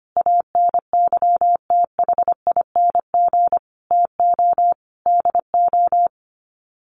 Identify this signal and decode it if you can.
Morse code, heard as a keyed tone, about 25 wpm, 700 Hz. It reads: ANYTHING TO DO
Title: Sets of 5 Words Encore 25wpm